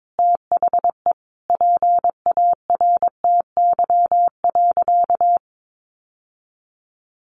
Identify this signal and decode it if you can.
Morse code, heard as a keyed tone, about 22 wpm, 700 Hz. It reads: THE PARTY.
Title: Sentences from Top 1500 Words 22wpm